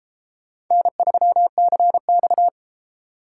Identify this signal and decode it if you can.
Morse code, heard as a keyed tone, about 33 wpm, 700 Hz. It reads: N3CX